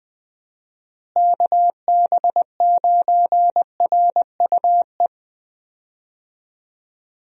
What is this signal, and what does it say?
Morse code, heard as a keyed tone, about 21 wpm, 700 Hz. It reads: KB9RUE
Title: USA Call Signs - 6 Characters - Repeated 3x - 20wpm